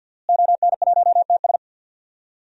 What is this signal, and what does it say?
Morse code, heard as a keyed tone, about 46 wpm, 700 Hz. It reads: ON1TS
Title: DXCC-100 Call Signs - 5 Characters - Repeated 2x - 50wpm